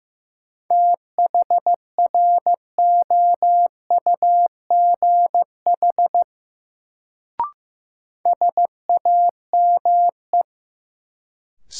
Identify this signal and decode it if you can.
Morse code, heard as a keyed tone, about 15 wpm, 700 Hz. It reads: THROUGH SAME
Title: Sentences from Top 200 Words - New Word Drill - 15wpm